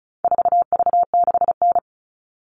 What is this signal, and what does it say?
Morse code, heard as a keyed tone, about 34 wpm, 700 Hz. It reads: 4V6N
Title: All Cut Numbers - 4-Digit 35wpm